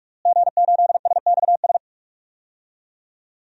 Morse code, heard as a keyed tone, about 46 wpm, 700 Hz. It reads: G8SXS
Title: DXCC-100 Call Signs - 5 Characters - Repeated 3x - 45wpm